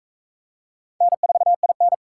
Morse code, heard as a keyed tone, about 41 wpm, 700 Hz. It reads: N4IN